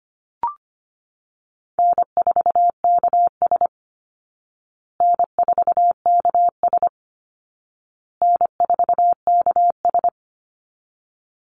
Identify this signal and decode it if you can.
Morse code, heard as a keyed tone, about 25 wpm, 700 Hz. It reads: N4KS N4KS N4KS
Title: Easy Call Signs Repeated 3x - 25wpm